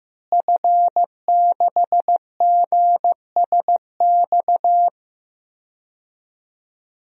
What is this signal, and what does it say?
Morse code, heard as a keyed tone, about 15 wpm, 700 Hz. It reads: F6GSX
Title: DXCC-100 Call Signs - 5 Characters - Repeated 2x - 15wpm